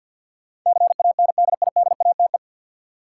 Morse code, heard as a keyed tone, about 50 wpm, 700 Hz. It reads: CANDIDATE
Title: Sentences from Top 1600 Words - New Word Drill Alt - 50wpm